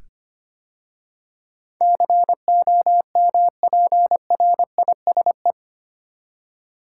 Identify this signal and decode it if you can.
Morse code, heard as a keyed tone, about 25 wpm, 700 Hz. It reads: COMPRISE
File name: 8 Letter Words - First 3 Letters Given - 25wpm